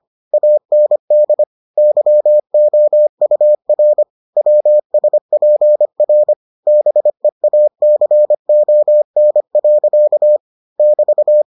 Morse code, heard as a keyed tone, about 25 wpm, 600 Hz. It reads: AND YOUR WSPR BEACON. <BT>